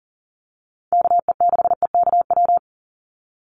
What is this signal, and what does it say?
Morse code, heard as a keyed tone, about 38 wpm, 700 Hz. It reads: KE6EKW
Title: USA Call Signs 40wpm